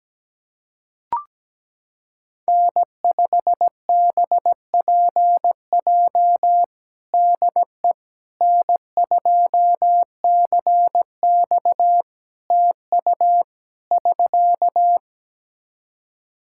Morse code, heard as a keyed tone, about 17 wpm, 700 Hz. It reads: N5BPJ DE N2CX TU <SK>
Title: K1USN SST QSO Practice 17wpm